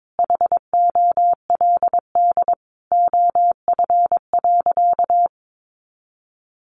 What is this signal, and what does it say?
Morse code, heard as a keyed tone, about 22 wpm, 700 Hz. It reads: HOLD OF.
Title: A Sound Of Thunder 22wpm